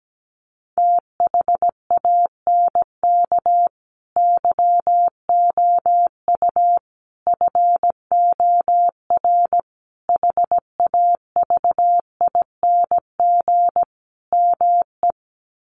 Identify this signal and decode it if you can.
Morse code, heard as a keyed tone, about 17 wpm, 700 Hz. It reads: THANK YOU FOR HAVING ME
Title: Sets of 5 Words Encore 17wpm